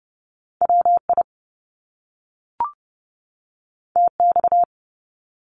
Morse code, heard as a keyed tone, about 30 wpm, 700 Hz. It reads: WI TX
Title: US States 30wpm